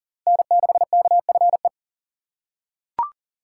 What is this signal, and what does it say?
Morse code, heard as a keyed tone, about 40 wpm, 700 Hz. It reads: N6KFE
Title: USA Call Signs - 5 Characters - 3x Character Spacing - 40wpm